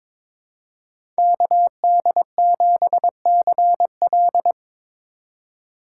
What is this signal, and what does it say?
Morse code, heard as a keyed tone, about 22 wpm, 700 Hz. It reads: KD7CL
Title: USA Call Signs - 5 Characters - 2x Character Spacing - 22wpm